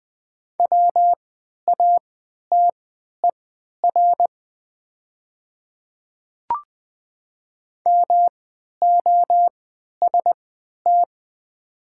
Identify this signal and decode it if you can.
Morse code, heard as a keyed tone, about 20 wpm, spaced as for 11 wpm, 700 Hz. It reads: WATER MOST
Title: 3x Character Spacing - Top 100 Words - 20wpm